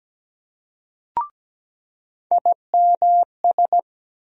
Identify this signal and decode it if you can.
Morse code, heard as a keyed tone, about 17 wpm, 700 Hz. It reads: IMS
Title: USA Call Sign Postfixes - 3 Letters 17wpm